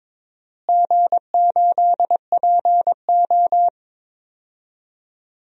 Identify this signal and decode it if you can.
Morse code, heard as a keyed tone, about 21 wpm, 700 Hz. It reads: G8PO
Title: DXCC-100 Call Signs - 4 Characters - Repeated 3x - 22wpm